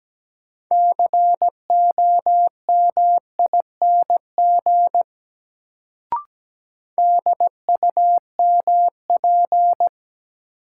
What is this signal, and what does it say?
Morse code, heard as a keyed tone, about 17 wpm, 700 Hz. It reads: COMING DUMP
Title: Sentences from Top 1300 Words - New Word Drill - 17wpm